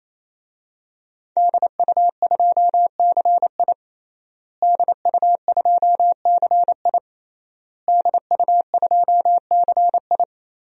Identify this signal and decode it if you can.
Morse code, heard as a keyed tone, about 28 wpm, 700 Hz. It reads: DU2CI DU2CI DU2CI
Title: DXCC-100 Call Signs - 5 Characters - Repeated 3x - 28wpm